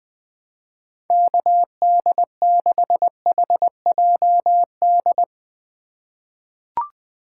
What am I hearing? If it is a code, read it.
Morse code, heard as a keyed tone, about 21 wpm, 700 Hz. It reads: KD6HJD
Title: USA Call Signs - 6 Characters - 2x Character Spacing - 20wpm